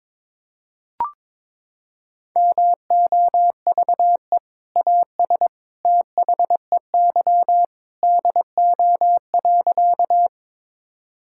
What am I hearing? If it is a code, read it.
Morse code, heard as a keyed tone, about 22 wpm, 700 Hz. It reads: MOVE AS THEY DO.
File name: practice / 22wpm / Sentences with 2-Word Phrases - 22wpm